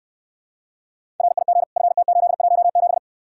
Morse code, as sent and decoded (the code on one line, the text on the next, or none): .-. .. --. / ..-. - ---.. .---- --...
RIG FT817